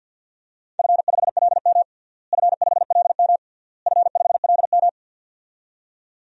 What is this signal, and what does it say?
Morse code, heard as a keyed tone, about 48 wpm, 700 Hz. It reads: F5LM F5LM F5LM
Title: DXCC-100 Call Signs - 4 Characters - Repeated 3x - 50wpm